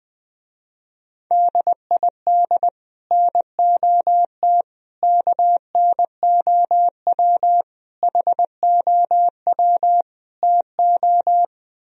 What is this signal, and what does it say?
Morse code, heard as a keyed tone, about 20 wpm, 700 Hz. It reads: DID NOT KNOW HOW TO
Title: Speed-Racing - Sets of 5 Words 17wpm